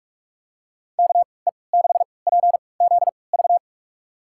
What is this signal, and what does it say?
Morse code, heard as a keyed tone, about 43 wpm, spaced as for 25 wpm, 700 Hz. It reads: KE6PZV
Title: USA Call Signs - 6 Characters - 3x Character Spacing - 45wpm